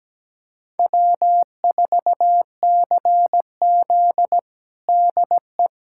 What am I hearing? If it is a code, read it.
Morse code, heard as a keyed tone, about 17 wpm, 700 Hz. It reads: W4CZ DE
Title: USA Callsign DE Callsign - 4 Characters - 17wpm